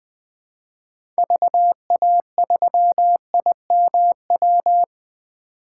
Morse code, heard as a keyed tone, about 20 wpm, 700 Hz. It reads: VA3IMW
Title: 2023 POTA Hunter Call Signs - 20wpm